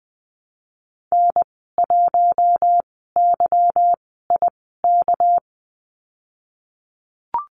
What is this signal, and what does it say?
Morse code, heard as a keyed tone, about 20 wpm, 700 Hz. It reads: N 1 Y I K
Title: 238 Lesson 38g - Letter q - Call Signs 20wpm